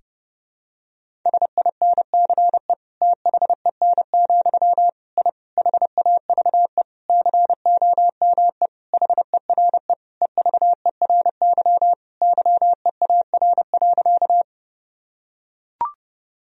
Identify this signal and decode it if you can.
Morse code, heard as a keyed tone, about 30 wpm, 700 Hz. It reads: SINCE THEN, I HAVE COME HERE EVERY YEAR.
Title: Sentences from Top 1000 Words - Review Part 1 - 30wpm